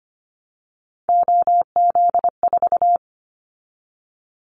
Morse code, heard as a keyed tone, about 25 wpm, 700 Hz. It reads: OZ4
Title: Callsign Prefixes for Top 100 DXCC - 25wpm